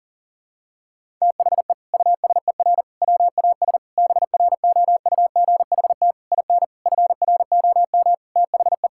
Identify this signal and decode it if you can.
Morse code, heard as a keyed tone, about 38 wpm, 700 Hz. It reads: THE USER WAS BROUGHT IN FROM THE
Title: Sentences from Top 1000 Words - Review Part 5 - 40wpm